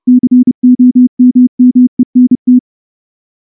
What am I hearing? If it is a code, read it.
Morse code, heard as a keyed tone, about 29 wpm, 250 Hz. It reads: COMMENT
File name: morse / morse-training-30WPM